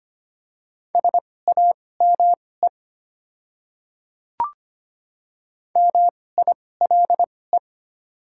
Morse code, heard as a keyed tone, about 25 wpm, spaced as for 18 wpm, 700 Hz. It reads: SAME MILE
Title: 2x Character Spacing - Top 300 Words - 25wpm